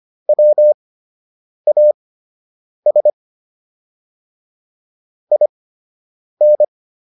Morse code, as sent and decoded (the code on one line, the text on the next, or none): .-- .- ... / .. -.
WAS IN